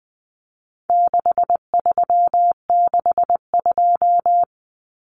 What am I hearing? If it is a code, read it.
Morse code, heard as a keyed tone, about 20 wpm, 700 Hz. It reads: 6362
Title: Numbers - 4-Digit 20wpm